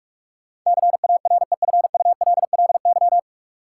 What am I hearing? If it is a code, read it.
Morse code, heard as a keyed tone, about 43 wpm, 700 Hz. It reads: CAREFULLY
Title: Sentences from Top 1300 Words - New Word Drill Alt - 45wpm